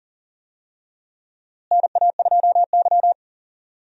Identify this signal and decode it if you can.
Morse code, heard as a keyed tone, about 40 wpm, 700 Hz. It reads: NA2Y
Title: USA Call Signs - 4 Characters - Repeated 3x - 40wpm